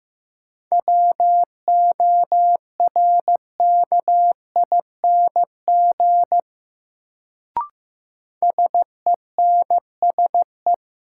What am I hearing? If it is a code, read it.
Morse code, heard as a keyed tone, about 15 wpm, 700 Hz. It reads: WORKING SENSE
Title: Sentences from Top 700 Words - New Word Drill - 15wpm